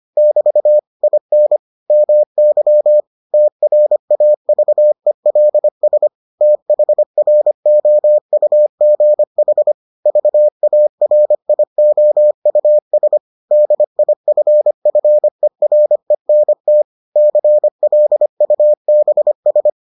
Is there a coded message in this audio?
Yes: Morse code from 0.2 to 19.7 s, heard as a keyed tone, about 25 wpm, 600 Hz. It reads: <BT> IN MY TRAVELS THROUGH VARIOUS DIFFERENT CLUBS